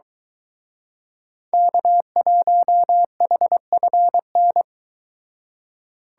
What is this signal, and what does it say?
Morse code, heard as a keyed tone, about 23 wpm, 700 Hz. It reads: K1HFN